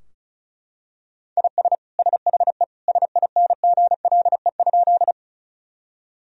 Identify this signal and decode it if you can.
Morse code, heard as a keyed tone, about 36 wpm, 700 Hz. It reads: IS SHE SINGLE?
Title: Sentences from Top 1000 Words - Review Part 4 - 35wpm